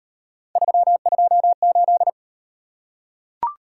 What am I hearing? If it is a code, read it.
Morse code, heard as a keyed tone, about 31 wpm, 700 Hz. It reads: 328
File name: Speed-Racing - Three Letter-Number Combinations - 25wpm